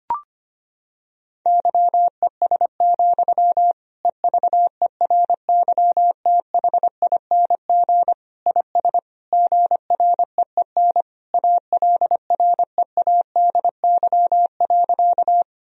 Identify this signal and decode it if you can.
Morse code, heard as a keyed tone, about 25 wpm, 700 Hz. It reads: YES, EVERYTHING IS GREEN ALREADY.
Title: Sentences from Top 1000 Words - Review Part 2 - 25wpm